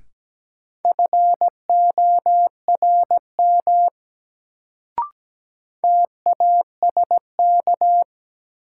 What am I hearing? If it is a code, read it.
Morse code, heard as a keyed tone, about 17 wpm, 700 Hz. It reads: FORM TASK
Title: Sentences from Top 1100 Words - New Word Drill Alt - 17wpm